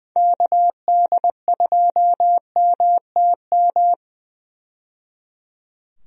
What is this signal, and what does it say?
Morse code, heard as a keyed tone, about 20 wpm, 700 Hz. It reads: KD2MTM
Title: USA Call Signs - 6 Characters - 20wpm